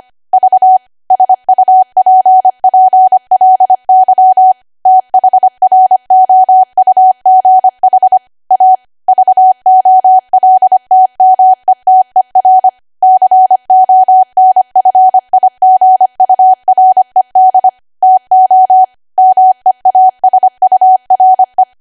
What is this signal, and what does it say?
Morse code, heard as a keyed tone, about 25 wpm, 750 Hz. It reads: V SUPPLY THROUGH A VOLTMETER CONFIGURED TO MEASURE